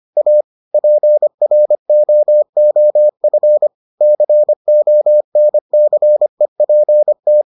Morse code, heard as a keyed tone, about 25 wpm, 600 Hz. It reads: A PROOF CONCEPT